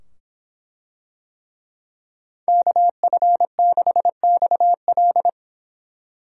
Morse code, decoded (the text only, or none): KF6XL